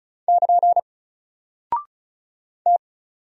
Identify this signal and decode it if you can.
Morse code, heard as a keyed tone, about 34 wpm, 700 Hz. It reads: <KN> T
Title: Single Character (Letter:Number:Punctuation:Prosign) 35wpm